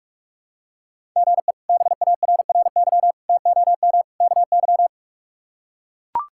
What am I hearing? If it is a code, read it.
Morse code, heard as a keyed tone, about 41 wpm, 700 Hz. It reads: GE BARRY TOM KY
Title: K1USN SST QSO Practice - 45wpm